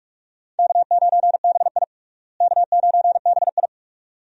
Morse code, heard as a keyed tone, about 46 wpm, 700 Hz. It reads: K9BI K9BI
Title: USA Call Signs - 4 Characters - Repeated 2x - 45wpm